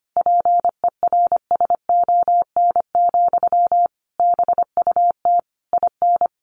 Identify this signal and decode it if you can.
Morse code, heard as a keyed tone, about 25 wpm, 700 Hz. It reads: PERSON, BUT IN
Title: Sentences from Top 1400 Words 25wpm